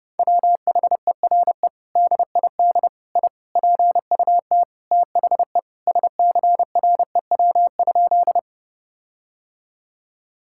Morse code, heard as a keyed tone, about 32 wpm, 700 Hz. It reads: WHERE DID I PUT THE SCREW?
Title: Sentences from Top 1600 Words 30wpm